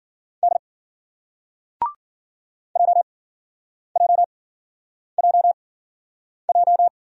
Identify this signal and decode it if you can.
Morse code, heard as a keyed tone, about 51 wpm, spaced as for 9 wpm, 700 Hz. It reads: D JJJJ